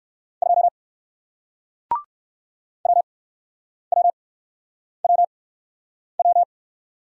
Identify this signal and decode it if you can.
Morse code, heard as a keyed tone, about 57 wpm, spaced as for 9 wpm, 700 Hz. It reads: 2 WWWW